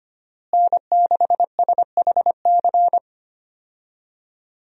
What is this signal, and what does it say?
Morse code, heard as a keyed tone, about 25 wpm, 700 Hz. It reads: N6SHC
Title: USA Call Signs - 5 Characters - Repeated 3x - 25wpm